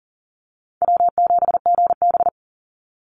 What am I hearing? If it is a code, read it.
Morse code, heard as a keyed tone, about 40 wpm, 700 Hz. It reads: W7GB
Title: Call Signs - 40wpm